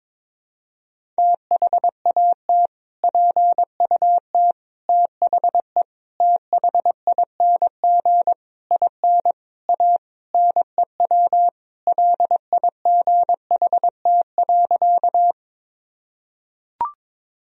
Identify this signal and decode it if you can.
Morse code, heard as a keyed tone, about 22 wpm, 700 Hz. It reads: THAT PUT THE THING IN A NEW LIGHT.
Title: Sentences from Top 1000 Words - Review Part 1 - 22wpm